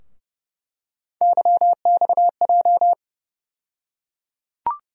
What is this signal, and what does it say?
Morse code, heard as a keyed tone, about 32 wpm, 700 Hz. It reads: YXJ
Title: USA Call Sign Postfixes - 3 Letters 30wpm